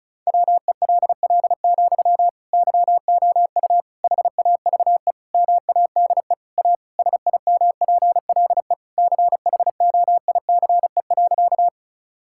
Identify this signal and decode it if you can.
Morse code, heard as a keyed tone, about 34 wpm, 700 Hz. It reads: WELL, YOU HAVE MADE A SIMPLE CHOICE.
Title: Sentences from Top 1000 Words - Review Part 6 - 35wpm